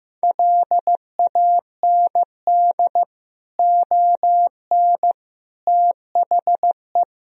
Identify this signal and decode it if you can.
Morse code, heard as a keyed tone, about 15 wpm, 700 Hz. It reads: LAND ON THE
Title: Sentences from Top 1000 Words - Review Part 1 - 15wpm